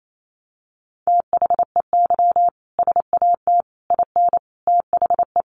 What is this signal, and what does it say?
Morse code, heard as a keyed tone, about 28 wpm, 700 Hz. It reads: THEY SAT IN THE
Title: A Sound Of Thunder 28wpm